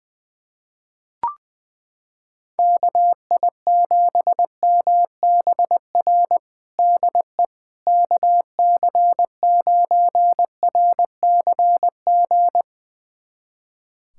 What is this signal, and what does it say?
Morse code, heard as a keyed tone, about 20 wpm, 700 Hz. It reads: KI7MBR DE KC9RCG
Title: USA Callsign DE Callsign - 6 Characters - 20wpm